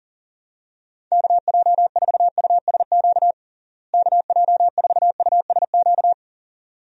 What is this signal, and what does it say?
Morse code, heard as a keyed tone, about 40 wpm, 700 Hz. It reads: KJ4USQ KJ4USQ
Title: USA Call Signs - 6 Characters - Repeated 2x - 40wpm